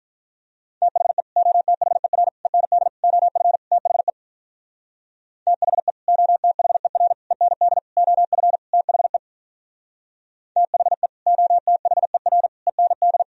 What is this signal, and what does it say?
Morse code, heard as a keyed tone, about 45 wpm, 700 Hz. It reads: THE OTHER END OF THE THE OTHER END OF THE THE OTHER END